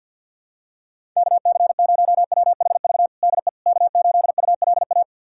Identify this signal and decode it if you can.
Morse code, heard as a keyed tone, about 51 wpm, 700 Hz. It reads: KC0WHV DE K8ULA
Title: USA Callsign DE Callsign 50wpm